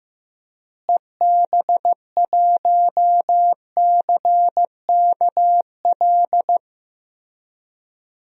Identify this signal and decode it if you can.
Morse code, heard as a keyed tone, about 15 wpm, 700 Hz. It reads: EB1CKL
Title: DXCC-100 Call Signs - 6 Characters - Repeated 2x - 15wpm